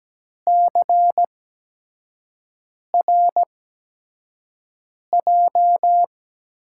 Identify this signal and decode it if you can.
Morse code, heard as a keyed tone, about 17 wpm, spaced as for 5 wpm, 700 Hz. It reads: CRJ
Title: USA Call Signs - 5 Characters - 8x Character Spacing - 17wpm